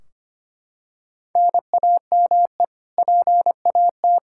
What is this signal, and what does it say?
Morse code, heard as a keyed tone, about 25 wpm, 700 Hz. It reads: NAME PAT
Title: Practice QSO - Exchange 1 - Beginning - 25wpm